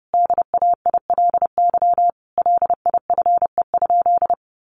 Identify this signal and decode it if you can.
Morse code, heard as a keyed tone, about 28 wpm, 700 Hz. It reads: DAILY LIFE?
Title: Sentences from Top 1100 Words 30wpm